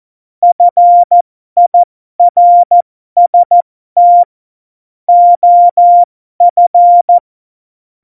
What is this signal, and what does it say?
Morse code, heard as a keyed tone, about 14 wpm, spaced as for 12 wpm, 700 Hz. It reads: FIRST OF